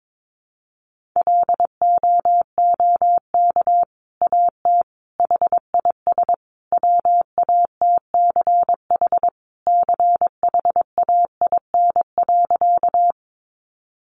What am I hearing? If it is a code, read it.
Morse code, heard as a keyed tone, about 22 wpm, 700 Hz. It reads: LOOK AT HIS WATCH CHAIN.
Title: Sentences from Top 1500 Words 22wpm